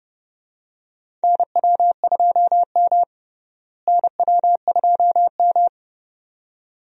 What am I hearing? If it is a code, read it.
Morse code, heard as a keyed tone, about 30 wpm, 700 Hz. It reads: NW2M NW2M
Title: USA Call Signs - 4 Characters - Repeated 2x - 30wpm